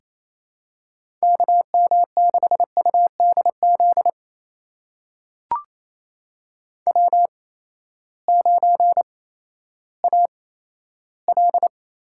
Morse code, heard as a keyed tone, about 28 wpm, spaced as for 14 wpm, 700 Hz. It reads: KM6UDZ W9AL
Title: USA Call Signs - 8x Character Spacing - 28wpm